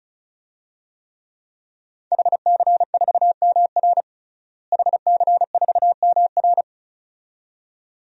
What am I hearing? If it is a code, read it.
Morse code, heard as a keyed tone, about 34 wpm, 700 Hz. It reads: HC4MR HC4MR
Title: DXCC-100 Call Signs - 5 Characters - Repeated 2x - 35wpm